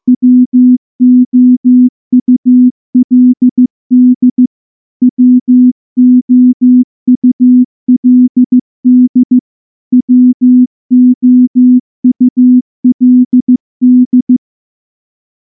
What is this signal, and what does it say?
Morse code, heard as a keyed tone, about 14 wpm, 250 Hz. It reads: WOULD WOULD WOULD
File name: morse / morse-training-15WPM